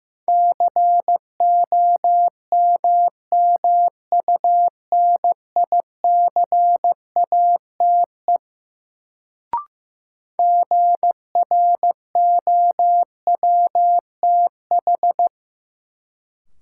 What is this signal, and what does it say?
Morse code, heard as a keyed tone, about 16 wpm, 700 Hz. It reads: COMMUNICATE GROWTH
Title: Sentences from Top 1300 Words - New Word Drill Alt - 15wpm